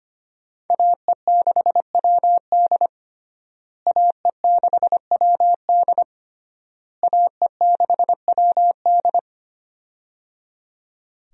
Morse code, heard as a keyed tone, about 25 wpm, 700 Hz. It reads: AE6WD AE6WD AE6WD
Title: USA Call Signs - 5 Characters - Repeated 3x - 25wpm